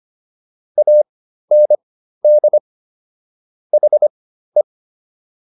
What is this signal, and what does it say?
Morse code, heard as a keyed tone, about 25 wpm, spaced as for 13 wpm, 600 Hz. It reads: AND HE